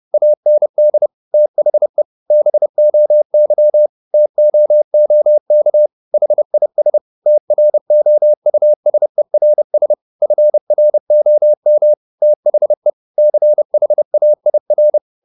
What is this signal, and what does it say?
Morse code, heard as a keyed tone, about 30 wpm, 600 Hz. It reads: AND THE BOY TOOK HIS TROUSERS FROM THE CHAIR
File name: the old man and the sea / the old man and the sea 1 30wpm